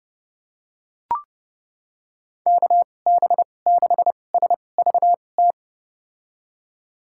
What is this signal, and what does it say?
Morse code, heard as a keyed tone, about 31 wpm, spaced as for 22 wpm, 700 Hz. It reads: KB6SVT
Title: USA Call Signs - 6 Characters - 2x Character Spacing - 30wpm